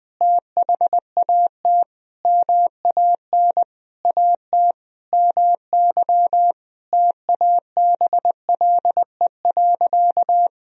Morse code, heard as a keyed tone, about 21 wpm, 700 Hz. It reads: THAT MAN AT MY TABLE.
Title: Sentences from Top 1000 Words - Review Part 2 - 20wpm